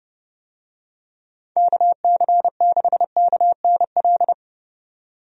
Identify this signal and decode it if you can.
Morse code, heard as a keyed tone, about 30 wpm, 700 Hz. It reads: KC6KNL
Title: USA Call Signs - 6 Characters - 30wpm